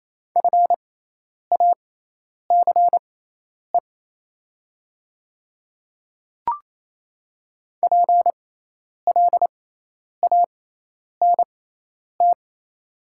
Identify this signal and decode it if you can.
Morse code, heard as a keyed tone, about 28 wpm, spaced as for 10 wpm, 700 Hz. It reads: FACE PLANT
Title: 6x Character Spacing - Top 300 Words - 28wpm